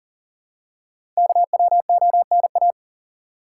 Morse code, heard as a keyed tone, about 40 wpm, 700 Hz. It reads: KWONA
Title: Code Groups - 5 Letters - 40wpm